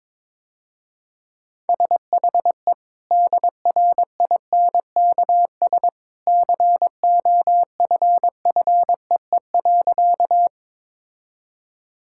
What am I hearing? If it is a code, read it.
Morse code, heard as a keyed tone, about 22 wpm, 700 Hz. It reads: SHE DRINKS COFFEE.
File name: practice / 22wpm / Easy Sentences with Hints - 22wpm